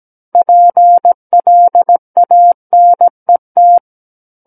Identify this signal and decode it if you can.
Morse code, heard as a keyed tone, about 17 wpm, 700 Hz. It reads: PLANET
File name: Book One / 08-Chapter 8 - 17 WPM